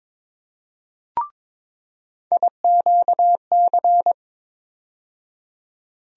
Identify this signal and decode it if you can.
Morse code, heard as a keyed tone, about 22 wpm, 700 Hz. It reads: IQC
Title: USA Call Sign Postfixes - 3 Letters 22wpm